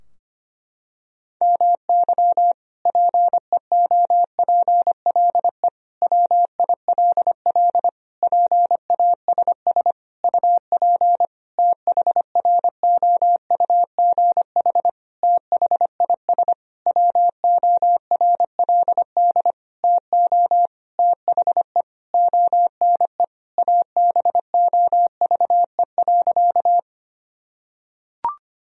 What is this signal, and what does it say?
Morse code, heard as a keyed tone, about 25 wpm, 700 Hz. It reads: MY PEOPLE WILL PASS UP THROUGH THIS WORLD TO THE ONE ABOVE.
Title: Sentences from Top 1000 Words - Review Part 2 - 25wpm